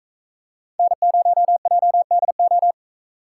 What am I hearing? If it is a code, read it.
Morse code, heard as a keyed tone, about 41 wpm, 700 Hz. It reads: N0JDO